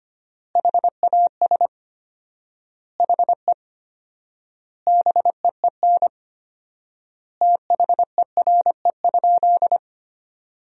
Morse code, heard as a keyed tone, about 25 wpm, 700 Hz. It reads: HAS HE BEEN THERE?
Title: 4x Word-Spacing - Sentences from Top 100 Words - 25wpm